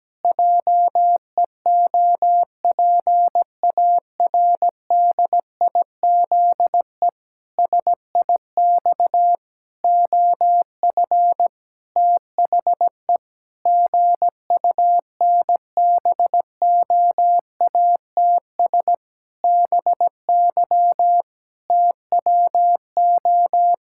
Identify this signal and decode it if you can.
Morse code, heard as a keyed tone, about 17 wpm, 700 Hz. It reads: JEOPARDIZE SIX OF THE GUNBOATS BY TWO
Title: Pangram Challenge v2 - 17wpm